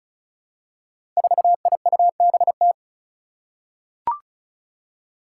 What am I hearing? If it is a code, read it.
Morse code, heard as a keyed tone, about 36 wpm, 700 Hz. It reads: 4IUBT
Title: Five Challenging Letter-Number Combinations 35wpm